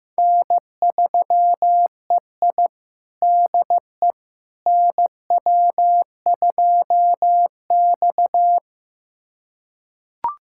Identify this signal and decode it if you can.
Morse code, heard as a keyed tone, about 15 wpm, 700 Hz. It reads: N3EI DE NW2X
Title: USA Callsign DE Callsign - 4 Characters - 15wpm